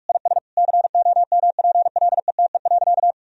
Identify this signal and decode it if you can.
Morse code, heard as a keyed tone, about 46 wpm, 700 Hz. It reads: IS COMPLETE.
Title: Sentences from Top 1000 Words - Review Part 2 - 45wpm